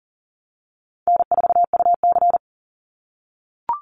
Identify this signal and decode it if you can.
Morse code, heard as a keyed tone, about 40 wpm, 700 Hz. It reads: N4UC
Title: Call Signs - 40wpm